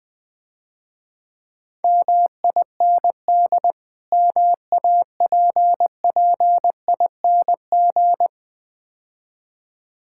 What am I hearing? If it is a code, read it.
Morse code, heard as a keyed tone, about 20 wpm, 700 Hz. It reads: MIND MAPPING
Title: Difficult Words - Speed-Racing with Hints - 20wpm